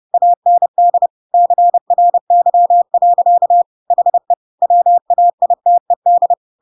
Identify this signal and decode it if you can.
Morse code, heard as a keyed tone, about 30 wpm, 700 Hz. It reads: AND CRY. HE WAITED
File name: Pinocchio / Pinocchio ch1 30 30 ik1hge